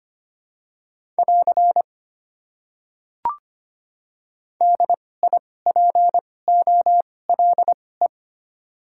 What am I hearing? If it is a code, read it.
Morse code, heard as a keyed tone, about 25 wpm, spaced as for 18 wpm, 700 Hz. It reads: <AR> DIPOLE
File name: Top 90 QSO Elements - 2x Character Spacing - 25wpm